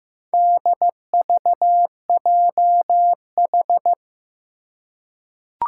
Morse code, heard as a keyed tone, about 15 wpm, 700 Hz. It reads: DVJH
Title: Four Challenging Letter-Number Combinations 15wpm